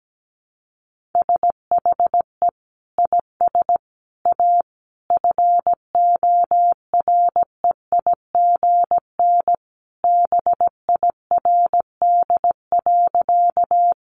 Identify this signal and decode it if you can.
Morse code, heard as a keyed tone, about 17 wpm, 700 Hz. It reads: SHE IS A FOREIGN BIRD.
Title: Sentences from Top 1400 Words 17wpm